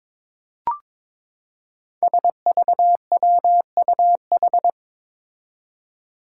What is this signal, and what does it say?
Morse code, heard as a keyed tone, about 22 wpm, 700 Hz. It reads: SVWUH
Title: Five Challenging Letter-Number Combinations 22wpm